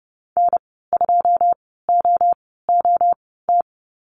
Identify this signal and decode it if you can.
Morse code, heard as a keyed tone, about 30 wpm, spaced as for 17 wpm, 700 Hz. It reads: N2OOT
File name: Lesson 15g - Number 5 - Call Signs 30wpm